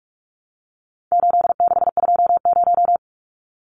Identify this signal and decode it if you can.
Morse code, heard as a keyed tone, about 44 wpm, 700 Hz. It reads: 8620
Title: Numbers - 4-Digit 45wpm